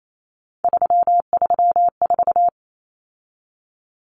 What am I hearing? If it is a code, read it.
Morse code, heard as a keyed tone, about 28 wpm, 700 Hz. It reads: 334
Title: Common Cut Numbers - 3-Digit 28wpm